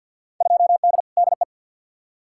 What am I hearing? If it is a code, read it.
Morse code, heard as a keyed tone, about 53 wpm, 700 Hz. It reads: 2D DE
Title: Field Day Exchanges Encore - 50wpm